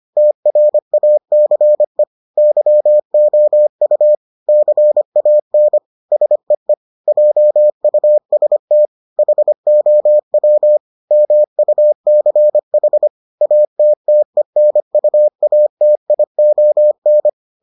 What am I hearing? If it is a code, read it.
Morse code, heard as a keyed tone, about 25 wpm, 600 Hz. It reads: TRACE YOU CAN SEE JUST HOW MUCH ATTENUATION